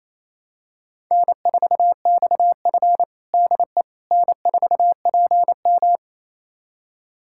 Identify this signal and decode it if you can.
Morse code, heard as a keyed tone, about 28 wpm, 700 Hz. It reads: N4XF DE N4PM
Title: USA Callsign DE Callsign - 4 Characters - 28wpm